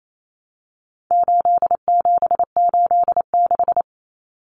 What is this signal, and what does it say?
Morse code, heard as a keyed tone, about 28 wpm, 700 Hz. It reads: 8786
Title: Numbers - 4-Digit 28wpm